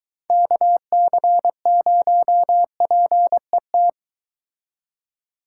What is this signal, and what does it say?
Morse code, heard as a keyed tone, about 23 wpm, 700 Hz. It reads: KC0PET